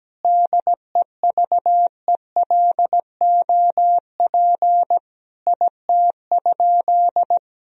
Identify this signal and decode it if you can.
Morse code, heard as a keyed tone, about 17 wpm, 700 Hz. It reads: DEVELOP IT?
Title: Sentences from Top 1000 Words - Review Part 2 - 17wpm